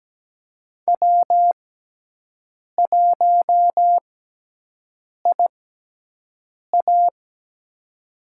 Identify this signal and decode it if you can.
Morse code, heard as a keyed tone, about 17 wpm, spaced as for 6 wpm, 700 Hz. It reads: W1IA
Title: USA Call Signs - 4 Characters - 6x Character Spacing - 17wpm